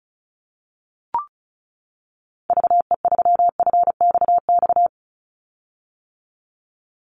Morse code, heard as a keyed tone, about 34 wpm, 700 Hz. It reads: VE3FXX
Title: Call Signs - 35wpm